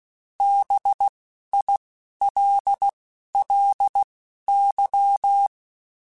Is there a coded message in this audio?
Yes: Morse code from 0.4 to 5.5 s, heard as a keyed tone, about 16 wpm, spaced as for 12 wpm, 800 Hz. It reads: BILLY